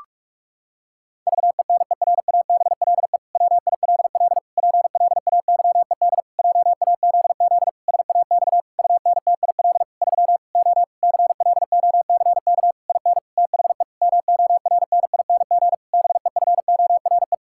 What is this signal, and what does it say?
Morse code, heard as a keyed tone, about 46 wpm, 700 Hz. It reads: VENERABLE WILL PLAYED JAZZ SAX UNTIL 3 O CLOCK IN THE MORNING BEFORE
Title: Pangram Challenge v2 - 45wpm